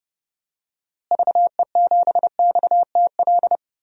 Morse code, heard as a keyed tone, about 30 wpm, 700 Hz. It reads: VE7XTL
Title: DXCC-100 Call Signs - 6 Characters - Repeated 3x - 30wpm